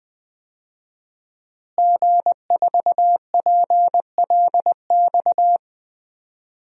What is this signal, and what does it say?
Morse code, heard as a keyed tone, about 21 wpm, 700 Hz. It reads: G4PLX
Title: DXCC-100 Call Signs - 5 Characters - Repeated 3x - 20wpm